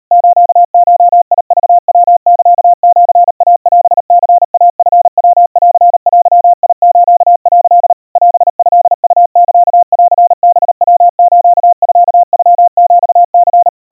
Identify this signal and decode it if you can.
Morse code (Japanese Wabun code), heard as a keyed tone, about 40 wpm, 700 Hz. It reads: スゴクヤサシイオニイチヤンデス」 オトウサンハヤスミノヒニ
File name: kawa / Morse sample / kamyu 140 cpm 700 Hz Crispy